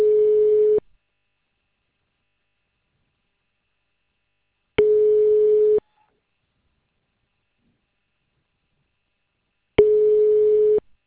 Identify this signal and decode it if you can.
Ringback tone (Europe, 425 Hz).